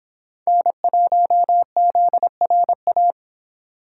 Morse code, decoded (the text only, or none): N1ZRA